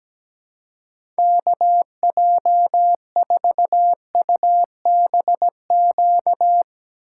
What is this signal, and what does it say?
Morse code, heard as a keyed tone, about 17 wpm, 700 Hz. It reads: KJ4UBQ
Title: USA Call Signs - 6 Characters - 3x Character Spacing - 17wpm